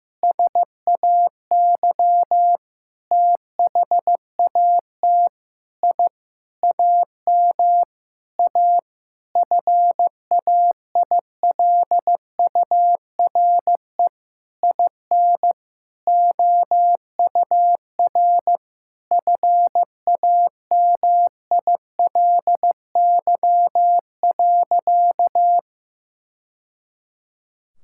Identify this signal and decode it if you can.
Morse code, heard as a keyed tone, about 15 wpm, 700 Hz. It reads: SAY THAT I AM A FAILURE IN OUR FAMILY.
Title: Sentences from Top 1000 Words - Review Part 6 - 15wpm